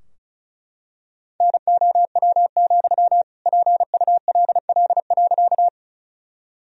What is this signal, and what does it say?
Morse code, heard as a keyed tone, about 34 wpm, 700 Hz. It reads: NOW, PULL.
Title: Sentences from Top 1000 Words - Review Part 2 - 35wpm